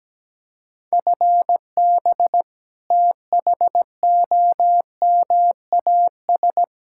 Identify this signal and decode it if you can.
Morse code, heard as a keyed tone, about 17 wpm, 700 Hz. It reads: FB THOMAS
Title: Practice QSO - Exchange 3 - More Info - 17wpm